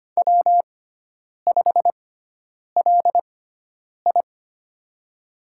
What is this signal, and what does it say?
Morse code, heard as a keyed tone, about 25 wpm, spaced as for 9 wpm, 700 Hz. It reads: W5LI
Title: USA Call Signs - 6x Character Spacing - 25wpm